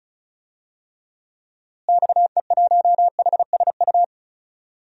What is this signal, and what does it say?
Morse code, heard as a keyed tone, about 36 wpm, 700 Hz. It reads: XE1HSU
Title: DXCC-100 Call Signs - 6 Characters - Repeated 3x - 35wpm